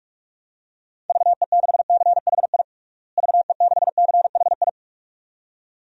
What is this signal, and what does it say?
Morse code, heard as a keyed tone, about 46 wpm, 700 Hz. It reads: VE6CHI VE6CHI
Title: DXCC-100 Call Signs - 6 Characters - Repeated 2x - 45wpm